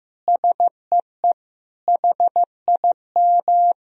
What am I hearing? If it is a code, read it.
Morse code, heard as a keyed tone, about 15 wpm, 700 Hz. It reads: SEE HIM
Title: Sentences from Top 1000 Words - Review Part 1 - 15wpm